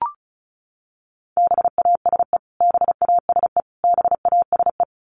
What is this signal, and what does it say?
Morse code, heard as a keyed tone, about 34 wpm, 700 Hz. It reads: BASE BASE BASE
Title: Top 500 Words Repeated 35wpm